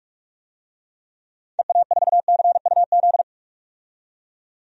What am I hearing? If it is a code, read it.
Morse code, heard as a keyed tone, about 42 wpm, 700 Hz. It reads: EA4CUZ
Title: DXCC-100 Call Signs - 6 Characters - 45wpm